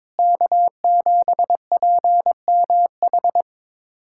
Morse code, heard as a keyed tone, about 21 wpm, 700 Hz. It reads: K7PMH
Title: USA Callsign DE Callsign - 5 Characters - 22wpm